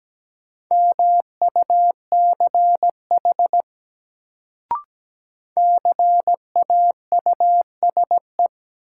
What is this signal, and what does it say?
Morse code, heard as a keyed tone, about 17 wpm, 700 Hz. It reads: MUCH CAUSE
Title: Sentences from Top 200 Words - New Word Drill - 17wpm